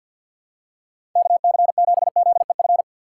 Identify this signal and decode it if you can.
Morse code, heard as a keyed tone, about 53 wpm, 700 Hz. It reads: KC7ZEF
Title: USA Call Signs - 6 Characters - 6x Character Spacing - 50wpm